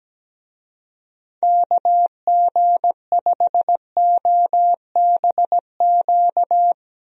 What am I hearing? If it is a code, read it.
Morse code, heard as a keyed tone, about 17 wpm, 700 Hz. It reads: KG5OBQ
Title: USA Call Signs - 6 Characters - Repeated 2x - 17wpm